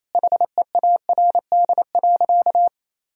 Morse code, heard as a keyed tone, about 28 wpm, 700 Hz. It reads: HEARD.
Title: Sentences from Top 400 Words and Abbreviations 28wpm